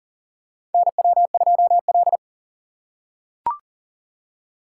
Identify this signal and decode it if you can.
Morse code, heard as a keyed tone, about 40 wpm, 700 Hz. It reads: NW2L
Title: USA Call Signs - 4 Characters - 4x Character Spacing - 40wpm